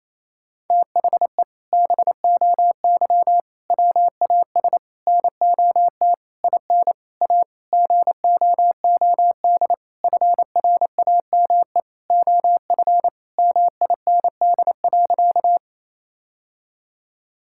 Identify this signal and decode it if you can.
Morse code, heard as a keyed tone, about 27 wpm, 700 Hz. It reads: THE BOY WAS NOT IN A GOOD FRAME OF MIND.
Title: Sentences from Top 1000 Words - Review Part 6 - 28wpm